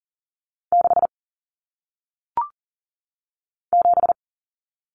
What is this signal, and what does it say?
Morse code, heard as a keyed tone, about 40 wpm, 700 Hz. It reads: 6 7
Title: Common Cut Numbers - 1-Digit 40wpm